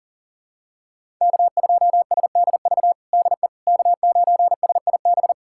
Morse code, heard as a keyed tone, about 40 wpm, 700 Hz. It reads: K2SDV DE K9SIB
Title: USA Callsign DE Callsign - 5 Characters - 40wpm